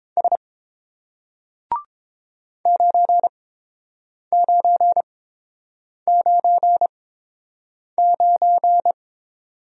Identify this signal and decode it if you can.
Morse code, heard as a keyed tone, about 31 wpm, spaced as for 7 wpm, 690 Hz. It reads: S 9999